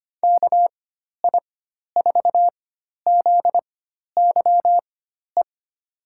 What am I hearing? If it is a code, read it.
Morse code, heard as a keyed tone, about 25 wpm, spaced as for 12 wpm, 700 Hz. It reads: KI4ZYE
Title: USA Call Signs - 6 Characters - 4x Character Spacing - 25wpm